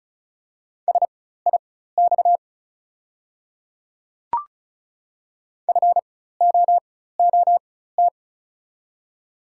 Morse code, heard as a keyed tone, about 34 wpm, spaced as for 16 wpm, 690 Hz. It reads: SIX FOOT
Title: 4x Character Spacing - Top 500 Words - 35wpm